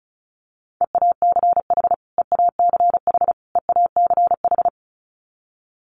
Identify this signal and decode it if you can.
Morse code, heard as a keyed tone, about 34 wpm, 700 Hz. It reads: EACH EACH EACH
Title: Top 500 Words Repeated 35wpm